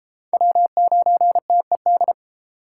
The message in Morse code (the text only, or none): W9TED